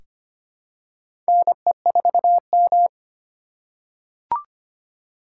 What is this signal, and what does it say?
Morse code, heard as a keyed tone, about 25 wpm, 700 Hz. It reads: NE4M
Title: K1USN SST QSO Practice - 25wpm